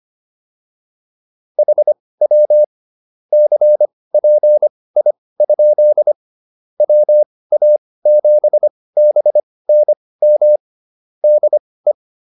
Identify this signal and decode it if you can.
Morse code, heard as a keyed tone, about 25 wpm, spaced as for 18 wpm, 600 Hz. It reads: HW CPI? WA7BNM DE
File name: QSO106 18